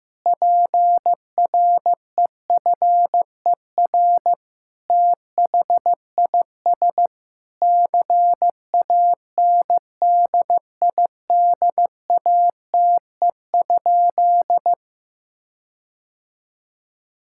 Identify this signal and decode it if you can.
Morse code, heard as a keyed tone, about 16 wpm, 700 Hz. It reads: PREFER THIS CANDIDATE?
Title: Sentences from Top 1600 Words 15wpm